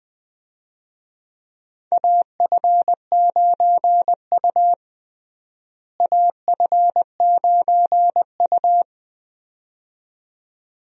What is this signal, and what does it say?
Morse code, heard as a keyed tone, about 20 wpm, 700 Hz. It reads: AF9U AF9U
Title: USA Call Signs - 4 Characters - Repeated 2x - 20wpm